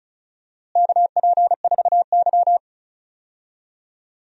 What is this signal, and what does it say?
Morse code, heard as a keyed tone, about 36 wpm, 700 Hz. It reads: KP4Y
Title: DXCC-100 Call Signs - 4 Characters - Repeated 2x - 35wpm